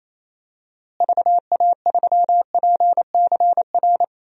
Morse code, heard as a keyed tone, about 27 wpm, 700 Hz. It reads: VA3PCR
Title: DXCC-100 Call Signs - 6 Characters - Repeated 3x - 28wpm